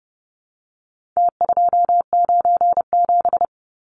Morse code, heard as a keyed tone, about 30 wpm, 700 Hz. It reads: T297
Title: Numbers - Contest Style - 4-Digit 30wpm